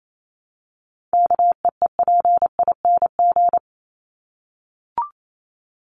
Morse code, heard as a keyed tone, about 28 wpm, 700 Hz. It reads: KEEPING
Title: ING Suffix - Top 100 Words with Base Words 28wpm